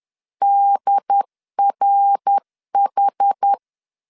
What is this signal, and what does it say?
Morse code, heard as a keyed tone, about 10 wpm, 800 Hz. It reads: DRH
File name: CW / DET / 1 6 1500C S C 10wpm 4252